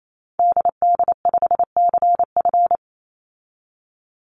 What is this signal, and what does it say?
Morse code, heard as a keyed tone, about 28 wpm, 700 Hz. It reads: DD5CF
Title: Call Signs Repeated 3x - 28wpm